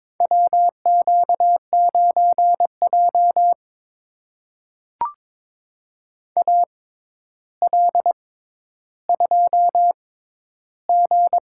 Morse code, heard as a keyed tone, about 22 wpm, spaced as for 11 wpm, 700 Hz. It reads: WQ9J AL2G
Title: USA Call Signs - 4 Characters - 6x Character Spacing - 22wpm